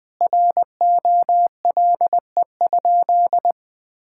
Morse code, heard as a keyed tone, about 21 wpm, 700 Hz. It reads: ROLE?
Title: Sentences from Top 1000 Words - Review Part 6 - 20wpm